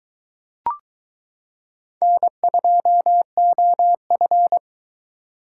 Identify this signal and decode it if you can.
Morse code, heard as a keyed tone, about 22 wpm, 700 Hz. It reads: N2OF